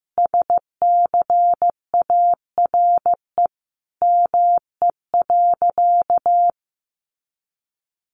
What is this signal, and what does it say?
Morse code, heard as a keyed tone, about 15 wpm, 700 Hz. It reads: SCARE ME.
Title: A Sound Of Thunder 15wpm